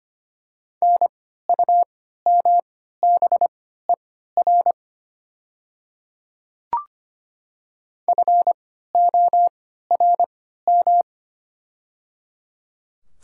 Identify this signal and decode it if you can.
Morse code, heard as a keyed tone, about 25 wpm, spaced as for 14 wpm, 700 Hz. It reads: NUMBER FORM
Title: 3x Character Spacing - Top 300 Words - 25wpm